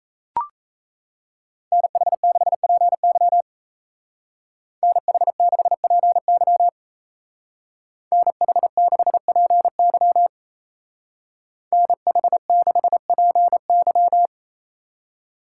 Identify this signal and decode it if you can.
Morse code, heard as a keyed tone, about 34 wpm, 700 Hz. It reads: NH6PY NH6PY NH6PY NH6PY
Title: Speed Racing - USA 2x2 Call Signs 28wpm